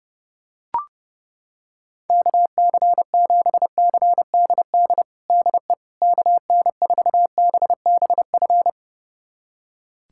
Morse code, heard as a keyed tone, about 32 wpm, 700 Hz. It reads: KC7CDD DE KN4BBF
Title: USA Callsign DE Callsign - 6 Characters - 30wpm